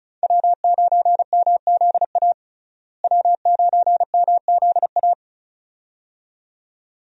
Morse code, heard as a keyed tone, about 36 wpm, 700 Hz. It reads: W9MZA W9MZA
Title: USA Call Signs - 5 Characters - Repeated 2x - 35wpm